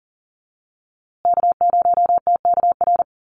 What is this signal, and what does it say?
Morse code, heard as a keyed tone, about 40 wpm, 700 Hz. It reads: K0TKR
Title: USA Call Signs Repeated 2x 40wpm